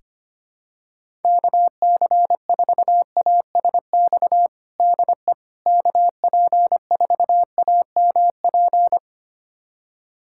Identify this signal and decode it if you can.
Morse code, heard as a keyed tone, about 25 wpm, 700 Hz. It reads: KC4ASX DE KP4AMP
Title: USA Callsign DE Callsign 25wpm